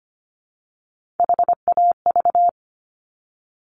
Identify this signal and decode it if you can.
Morse code, heard as a keyed tone, about 25 wpm, 700 Hz. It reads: HAV
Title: Top Three Letter Combinations in Words 25wpm